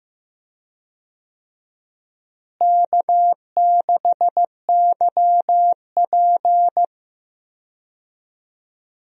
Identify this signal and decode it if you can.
Morse code, heard as a keyed tone, about 15 wpm, 700 Hz. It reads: K6YP
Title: USA Call Signs - 4 Characters - Repeated 2x - 15wpm